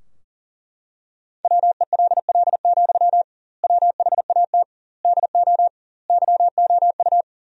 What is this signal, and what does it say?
Morse code, heard as a keyed tone, about 40 wpm, 700 Hz. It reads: WELL, WHAT DO YOU
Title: 2x Word-Spacing - Sentences from Top 200 Words - 40wpm